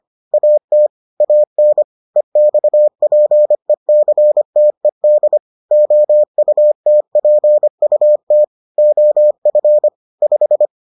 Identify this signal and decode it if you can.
Morse code, heard as a keyed tone, about 25 wpm, 600 Hz. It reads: AT AN EXPECTED OUTPUT OF 5